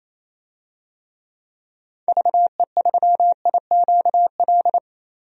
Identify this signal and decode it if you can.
Morse code, heard as a keyed tone, about 28 wpm, 700 Hz. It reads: VE3IQL
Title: DXCC-100 Call Signs - 6 Characters - 28wpm